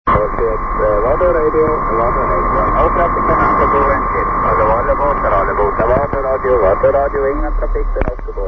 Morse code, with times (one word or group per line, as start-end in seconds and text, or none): none